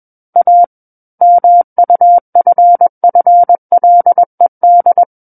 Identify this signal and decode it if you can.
Morse code, heard as a keyed tone, about 21 wpm, 700 Hz. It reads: A MUFFLED
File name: Book One / 12-Chapter 12 - 21 WPM